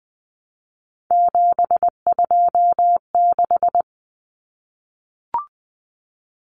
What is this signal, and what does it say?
Morse code, heard as a keyed tone, about 20 wpm, 700 Hz. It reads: 726
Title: Common Cut Numbers - 3-Digit 20wpm